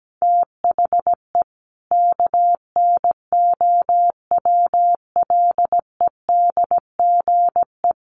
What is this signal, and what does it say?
Morse code, heard as a keyed tone, about 17 wpm, 700 Hz. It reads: THE KNOWLEDGE
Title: Sentences from Top 1400 Words 17wpm